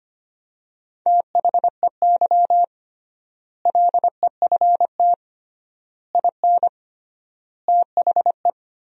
Morse code, heard as a keyed tone, about 25 wpm, 700 Hz. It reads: THEY LEFT IN THE
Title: 3x Word-Spacing - Sentences from Top 400 Words - 25wpm